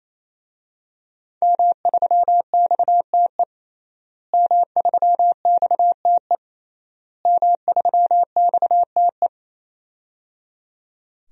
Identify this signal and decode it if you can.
Morse code, heard as a keyed tone, about 28 wpm, 700 Hz. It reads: M3XTE M3XTE M3XTE
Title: DXCC-100 Call Signs - 5 Characters - Repeated 3x - 28wpm